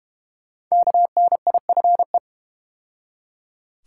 Morse code, heard as a keyed tone, about 32 wpm, 700 Hz. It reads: KNIFE